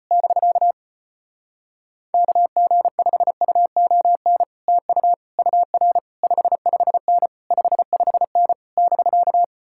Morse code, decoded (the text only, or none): <BK> KG5UON TU UR 55N 55N <BK>